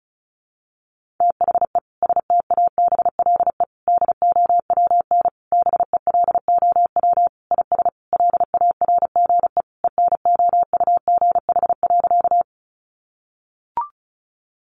Morse code, heard as a keyed tone, about 34 wpm, 700 Hz. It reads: THE STABLE DOWN BELOW IS LARGE ENOUGH.
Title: Sentences from Top 1500 Words 35wpm